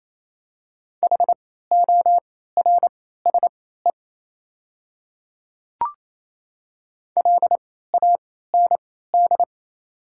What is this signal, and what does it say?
Morse code, heard as a keyed tone, about 28 wpm, spaced as for 16 wpm, 700 Hz. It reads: HORSE LAND
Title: 3x Character Spacing - Top 300 Words - 28wpm